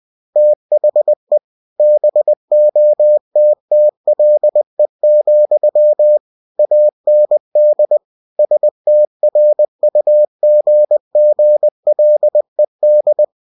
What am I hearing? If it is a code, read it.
Morse code, heard as a keyed tone, about 20 wpm, 600 Hz. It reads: THE BOTTLE, AND STRUGGLED
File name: cw-books / winnie-the-pooh / winnie-20wpm-page-0094